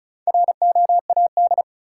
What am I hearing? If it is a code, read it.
Morse code, heard as a keyed tone, about 36 wpm, 700 Hz. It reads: ROAD
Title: Sets of 5 Words - New Word Drill - 35wpm